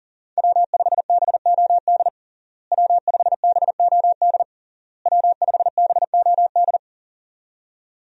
Morse code, heard as a keyed tone, about 40 wpm, 700 Hz. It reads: W5BOD W5BOD W5BOD
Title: USA Call Signs - 5 Characters - Repeated 3x - 40wpm